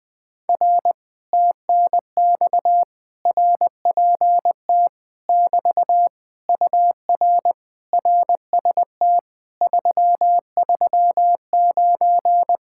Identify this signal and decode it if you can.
Morse code, heard as a keyed tone, about 20 wpm, 700 Hz. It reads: R TNX RPT <BT> UR RST 339
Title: Speed-Racing Common QSO Phrases - 20wpm